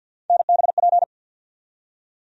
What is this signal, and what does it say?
Morse code, heard as a keyed tone, about 52 wpm, 690 Hz. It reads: NBP
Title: USA Call Sign Postfixes - 3 Letters 50wpm